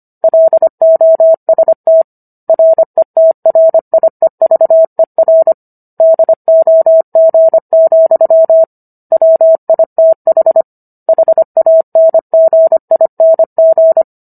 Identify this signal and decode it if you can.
Morse code, heard as a keyed tone, about 25 wpm, 650 Hz. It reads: LOST RETRIEVER DOG, WITH HANGING